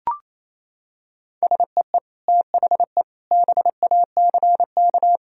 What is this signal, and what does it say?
Morse code, heard as a keyed tone, about 28 wpm, 700 Hz. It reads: SEE THE BACK
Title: Sentences from Top 1000 Words - Review Part 1 - 28wpm